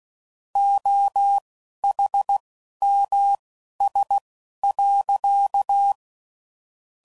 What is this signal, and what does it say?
Morse code, heard as a keyed tone, about 16 wpm, spaced as for 11 wpm, 800 Hz. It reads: OHMS.
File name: CW Code Groups @10WPM / Technical Subjects / XC